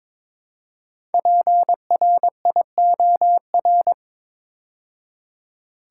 Morse code, heard as a keyed tone, about 22 wpm, 700 Hz. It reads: PRIOR
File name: 5 Letter Words - First 3 Letters Given - 22wpm